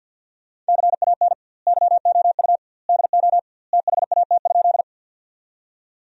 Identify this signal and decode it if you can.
Morse code, heard as a keyed tone, about 50 wpm, 700 Hz. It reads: CAN YOU DO THAT?
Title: 2x Word-Spacing - Sentences from Top 100 Words - 50wpm